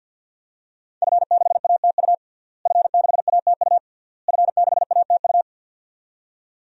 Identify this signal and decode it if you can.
Morse code, heard as a keyed tone, about 50 wpm, 700 Hz. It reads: F6ATU F6ATU F6ATU
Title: DXCC-100 Call Signs - 5 Characters - Repeated 3x - 50wpm